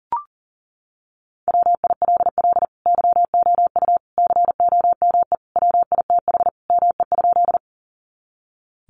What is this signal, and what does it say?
Morse code, heard as a keyed tone, about 40 wpm, 700 Hz. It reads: WILL YOU COME WITH ME?
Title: Sentences from Top 200 Words 40wpm